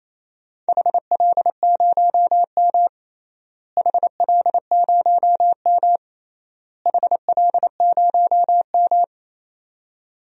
Morse code, heard as a keyed tone, about 28 wpm, 700 Hz. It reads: HL0M HL0M HL0M
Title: DXCC-100 Call Signs - 4 Characters - Repeated 3x - 28wpm